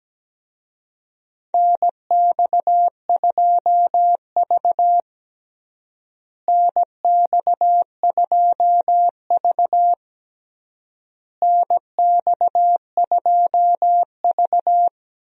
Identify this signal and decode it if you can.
Morse code, heard as a keyed tone, about 17 wpm, 700 Hz. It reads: NX2V NX2V NX2V
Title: USA Call Signs - 4 Characters - Repeated 3x - 17wpm